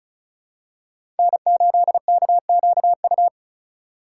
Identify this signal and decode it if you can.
Morse code, heard as a keyed tone, about 36 wpm, 700 Hz. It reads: N8KQU
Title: USA Call Signs - 5 Characters - 2x Character Spacing - 35wpm